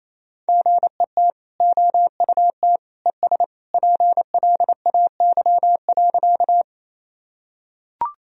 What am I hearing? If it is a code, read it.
Morse code, heard as a keyed tone, about 28 wpm, 700 Hz. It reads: GET OUT ES PLAY.
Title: Sentences from Top 200 Words and Abbreviations 28wpm